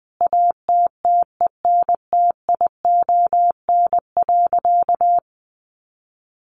Morse code, heard as a keyed tone, about 20 wpm, 700 Hz. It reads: ATTENTION.
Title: Sentences from Top 1400 Words 20wpm